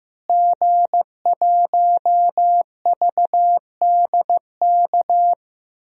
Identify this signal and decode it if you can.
Morse code, heard as a keyed tone, about 15 wpm, 700 Hz. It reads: G1VDK
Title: DXCC-100 Call Signs - 5 Characters - Repeated 3x - 15wpm